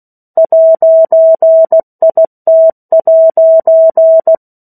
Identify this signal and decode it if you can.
Morse code, heard as a keyed tone, about 16 wpm, 650 Hz. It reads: 'IT'
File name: 03-Chapter-03 16 WPM